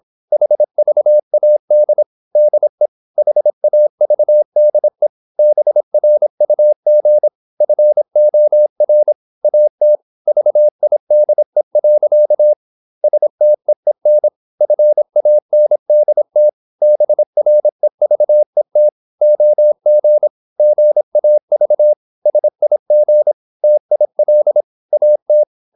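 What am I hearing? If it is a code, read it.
Morse code, heard as a keyed tone, about 26 wpm, 600 Hz. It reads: HVAD DE HAVDE BRUG FOR AT VIDE. STEEN FANDT BREVET OG GAV SIG TIL AT
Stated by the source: High-speed (QRQ) Morse at 26 wpm.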